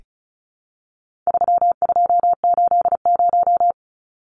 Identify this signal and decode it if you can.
Morse code, heard as a keyed tone, about 34 wpm, 700 Hz. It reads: 3280
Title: Numbers - 4-Digit 35wpm